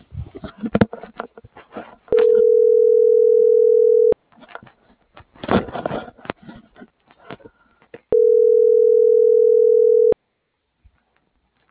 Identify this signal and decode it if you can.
Ringback tone (North America), 440 + 480 Hz.